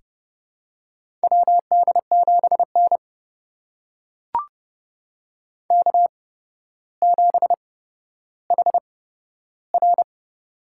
Morse code, heard as a keyed tone, about 30 wpm, spaced as for 13 wpm, 700 Hz. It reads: WD7N K7HR
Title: USA Call Signs - 4 Characters - 8x Character Spacing - 30wpm